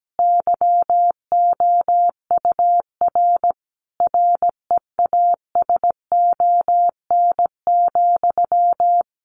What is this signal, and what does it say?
Morse code, heard as a keyed tone, about 17 wpm, 700 Hz. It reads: YOUR REASON,
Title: Sentences from Top 700 Words 17wpm